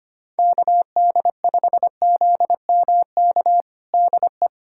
Morse code, heard as a keyed tone, about 25 wpm, 700 Hz. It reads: KD5ZMK DE
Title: USA Callsign DE Callsign - 6 Characters - 25wpm